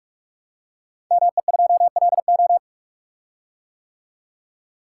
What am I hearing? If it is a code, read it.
Morse code, heard as a keyed tone, about 44 wpm, 700 Hz. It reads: ME2LO
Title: Code Groups - 5 Letter:Numbers - 45wpm